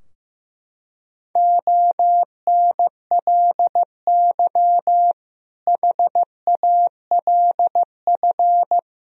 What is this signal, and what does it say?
Morse code, heard as a keyed tone, about 15 wpm, 700 Hz. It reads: ONLY HALF
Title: Sentences from Top 1000 Words - Review Part 2 - 15wpm